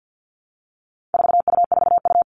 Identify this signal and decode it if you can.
Morse code, heard as a keyed tone, about 50 wpm, 700 Hz. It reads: 4U4U
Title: All Cut Numbers - 4-Digit 50wpm